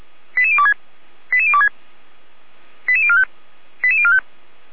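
Five-tone selective call (ZVEI), heard as quick runs of five tones, tones of about 70 ms. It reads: 88026 88026 88045 88045
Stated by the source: ZVEI five-tone call received on SDR, from the Feuerwehr Orlamünde (fire brigade) audio server.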